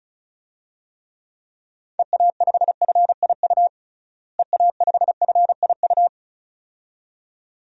Morse code, heard as a keyed tone, about 36 wpm, 700 Hz. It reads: EA5FIU EA5FIU
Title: DXCC-100 Call Signs - 6 Characters - Repeated 2x - 35wpm